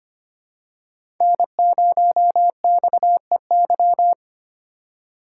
Morse code, heard as a keyed tone, about 25 wpm, 700 Hz. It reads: N0XEY
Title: USA Call Signs - 5 Characters - 4x Character Spacing - 25wpm